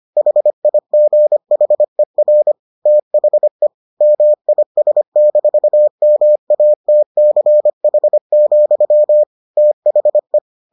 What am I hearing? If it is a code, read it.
Morse code, heard as a keyed tone, about 25 wpm, 600 Hz. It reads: HIGHER THE MIS-MATCH, THE